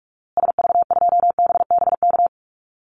Morse code, heard as a keyed tone, about 44 wpm, 700 Hz. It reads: SV2BBK
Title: Call Signs Repeated 2x - 45wpm